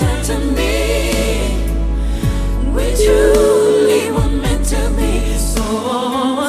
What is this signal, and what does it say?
Ringback tone (Europe, 425 Hz), with noise.